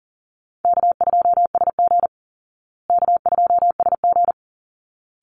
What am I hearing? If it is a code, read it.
Morse code, heard as a keyed tone, about 40 wpm, 700 Hz. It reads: K2SG K2SG
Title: Easy Call Signs Repeated 2x - 40wpm